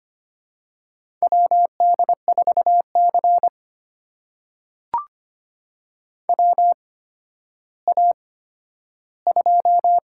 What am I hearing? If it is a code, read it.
Morse code, heard as a keyed tone, about 25 wpm, spaced as for 12 wpm, 700 Hz. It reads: WD4C WA2
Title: USA Call Signs - 8x Character Spacing - 25wpm